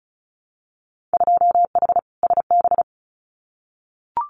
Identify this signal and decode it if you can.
Morse code, heard as a keyed tone, about 36 wpm, 700 Hz. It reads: 2H SB
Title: Winter Field Day Exchanges Encore 35wpm